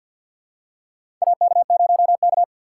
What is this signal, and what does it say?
Morse code, heard as a keyed tone, about 49 wpm, 700 Hz. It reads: AK0K
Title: USA Call Signs - 4 Characters - 3x Character Spacing - 50wpm